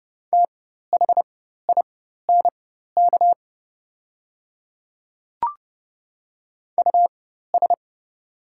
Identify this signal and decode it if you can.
Morse code, heard as a keyed tone, about 31 wpm, spaced as for 14 wpm, 700 Hz. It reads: THINK US
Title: 4x Character Spacing - Top 200 Words - 30wpm